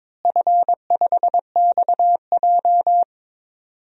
Morse code, heard as a keyed tone, about 23 wpm, 700 Hz. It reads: F5XJ
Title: DXCC-100 Call Signs - 4 Characters - Repeated 2x - 22wpm